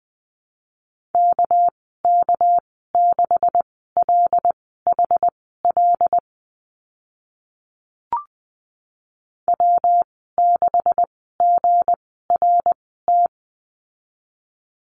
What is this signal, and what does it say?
Morse code, heard as a keyed tone, about 20 wpm, spaced as for 15 wpm, 700 Hz. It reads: KK6LHL W6GRT
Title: 219 Lesson 35g - Number 0 - Call Signs 20wpm